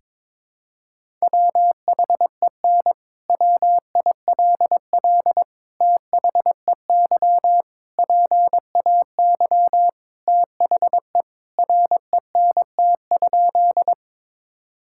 Morse code, heard as a keyed tone, about 21 wpm, 700 Hz. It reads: WHEN WILL THEY PAY THE RENT?
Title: Sentences from Top 1000 Words - Review Part 5 - 22wpm